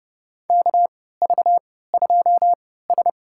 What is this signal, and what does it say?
Morse code, heard as a keyed tone, about 30 wpm, spaced as for 17 wpm, 700 Hz. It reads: KV2S
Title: USA Call Signs - 4 Characters - 3x Character Spacing - 30wpm